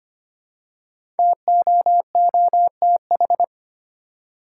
Morse code, heard as a keyed tone, about 25 wpm, 700 Hz. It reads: TOOTH
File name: 5 Letter Words - First 3 Letters Given - 25wpm